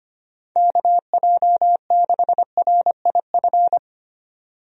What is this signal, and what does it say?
Morse code, heard as a keyed tone, about 25 wpm, 700 Hz. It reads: KJ6RIF
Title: USA Call Signs - 6 Characters - 25wpm